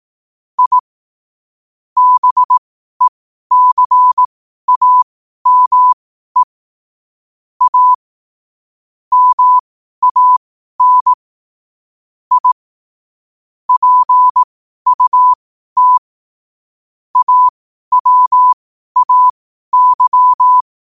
Morse code, heard as a keyed tone, about 18 wpm, spaced as for 12 wpm, 1000 Hz. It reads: I BECAME A MAN I PUT AWAY
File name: qotdfiles / 12wpm